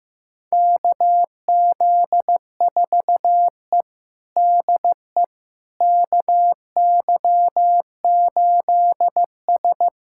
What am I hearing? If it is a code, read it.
Morse code, heard as a keyed tone, about 15 wpm, 700 Hz. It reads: KZ4E DE KY8S
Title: USA Callsign DE Callsign - 4 Characters - 15wpm